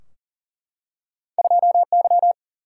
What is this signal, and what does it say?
Morse code, heard as a keyed tone, about 39 wpm, 700 Hz. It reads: 2Y
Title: Single-Letter Single-Number 40wpm